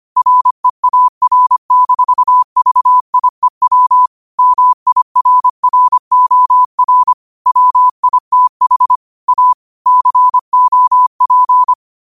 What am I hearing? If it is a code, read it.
Morse code, heard as a keyed tone, about 25 wpm, 1000 Hz. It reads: REAR-VIEW MIRROR WITH A COP
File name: qotdfiles / 25wpm